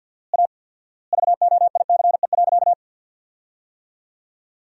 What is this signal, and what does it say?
Morse code, heard as a keyed tone, about 53 wpm, 700 Hz. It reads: A VOICE.
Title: 4x Word-Spacing - Sentences from Top 500 Words - 50wpm